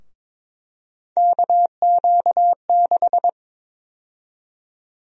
Morse code, heard as a keyed tone, about 23 wpm, 700 Hz. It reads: KQ6
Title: USA Call Sign Prefixes - 22wpm